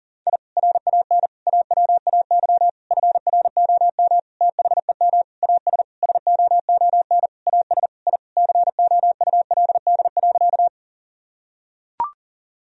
Morse code, heard as a keyed tone, about 40 wpm, 700 Hz. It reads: I RAN AWAY FROM THEM AS SOON AS I COULD.
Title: Sentences from Top 1000 Words - Review Part 3 - 40wpm